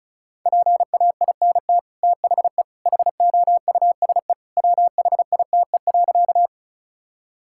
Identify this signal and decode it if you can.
Morse code, heard as a keyed tone, about 34 wpm, 700 Hz. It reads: PAINT THE HOUSE WHITE.
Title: Sentences from Top 1000 Words - Review Part 6 - 35wpm